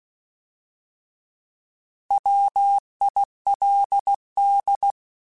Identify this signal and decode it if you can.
Morse code, heard as a keyed tone, about 16 wpm, 800 Hz. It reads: WILD